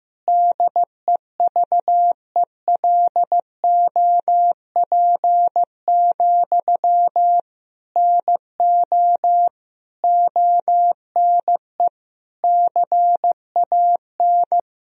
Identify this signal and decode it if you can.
Morse code, heard as a keyed tone, about 15 wpm, 700 Hz. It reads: DEVELOP, NO ONE CAN
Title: Sentences from Top 1000 Words - Review Part 2 - 15wpm